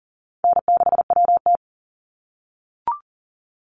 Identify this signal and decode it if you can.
Morse code, heard as a keyed tone, about 40 wpm, 700 Hz. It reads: N6WT
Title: NA SOTA Chaser Callsigns 40wpm